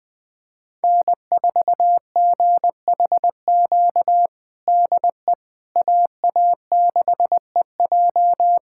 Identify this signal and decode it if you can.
Morse code, heard as a keyed tone, about 21 wpm, 700 Hz. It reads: N4GHQ DE AA6EJ
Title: USA Callsign DE Callsign - 5 Characters - 20wpm